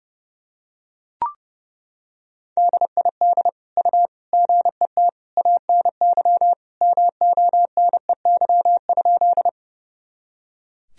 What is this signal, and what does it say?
Morse code, heard as a keyed tone, about 30 wpm, 700 Hz. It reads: DID U GET ANY MONEY?
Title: Sentences from Top 400 Words and Abbreviations 30wpm